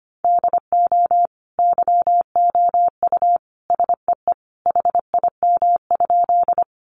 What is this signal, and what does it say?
Morse code, heard as a keyed tone, about 25 wpm, 700 Hz. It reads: DO YOU SEE HIM?
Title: Sentences from Top 100 Words 25wpm